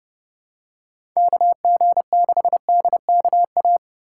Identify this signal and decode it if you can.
Morse code, heard as a keyed tone, about 30 wpm, 700 Hz. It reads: KG6DKA
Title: USA Call Signs - 4x Character Spacing - 30wpm